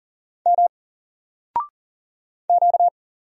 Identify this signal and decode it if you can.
Morse code, heard as a keyed tone, about 41 wpm, 700 Hz. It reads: M Q
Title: Single Letter-Number Alt 40wpm